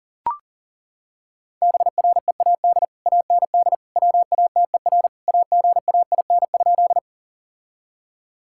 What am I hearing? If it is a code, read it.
Morse code, heard as a keyed tone, about 40 wpm, 700 Hz. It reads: BREAD AND WATER AGAIN?
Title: Sentences from Top 1600 Words 40wpm